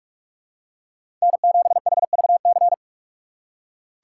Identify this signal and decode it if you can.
Morse code, heard as a keyed tone, about 46 wpm, 700 Hz. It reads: N7HVC
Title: USA Call Signs - 3x Character Spacing - 45wpm